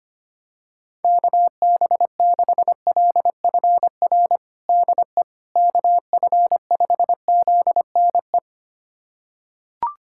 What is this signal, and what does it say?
Morse code, heard as a keyed tone, about 25 wpm, 700 Hz. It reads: KB6LFR DE KF5ZNE
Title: USA Callsign DE Callsign 25wpm